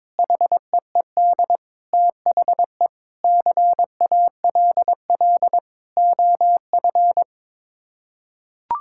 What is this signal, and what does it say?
Morse code, heard as a keyed tone, about 22 wpm, 700 Hz. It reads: HEED THE CALL OF
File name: practice / 22wpm / Sets of 8 Words - 22wpm